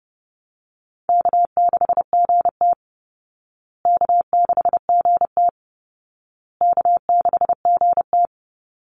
Morse code, heard as a keyed tone, about 30 wpm, 700 Hz. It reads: K6GT K6GT K6GT
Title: Easy Call Signs Repeated 3x - 30wpm